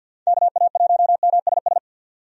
Morse code, heard as a keyed tone, about 50 wpm, 700 Hz. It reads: KA1MSS
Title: USA Call Signs - 6 Characters - 50wpm